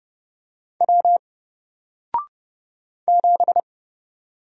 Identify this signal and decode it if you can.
Morse code, heard as a keyed tone, about 30 wpm, 700 Hz. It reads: W 7
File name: Single-Letter-Number-30wpm